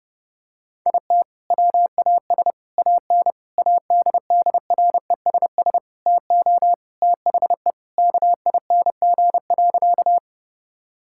Morse code, heard as a keyed tone, about 30 wpm, 700 Hz. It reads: IT WAS AN ADDRESS TO THE KING.
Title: Sentences from Top 1000 Words - Review Part 6 - 30wpm